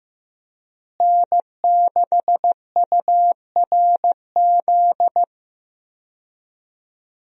Morse code, heard as a keyed tone, about 15 wpm, 700 Hz. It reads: N6URZ
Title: USA Call Signs - 5 Characters - Repeated 2x - 15wpm